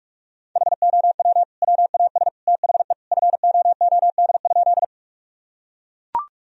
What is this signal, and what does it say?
Morse code, heard as a keyed tone, about 41 wpm, 700 Hz. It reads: HOW WAS THE FOOD?
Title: Sentences from Top 1000 Words - Review Part 1 - 45wpm